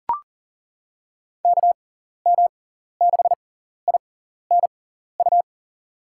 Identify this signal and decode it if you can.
Morse code, heard as a keyed tone, about 40 wpm, spaced as for 14 wpm, 700 Hz. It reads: KM6INU
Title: USA Call Signs - 6 Characters - 6x Character Spacing - 40wpm